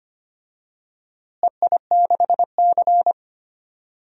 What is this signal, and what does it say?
Morse code, heard as a keyed tone, about 25 wpm, 700 Hz. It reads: EI6C
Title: DXCC-100 Call Signs - 4 Characters - Repeated 2x - 25wpm